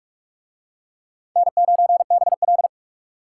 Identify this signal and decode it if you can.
Morse code, heard as a keyed tone, about 44 wpm, 700 Hz. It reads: N9BL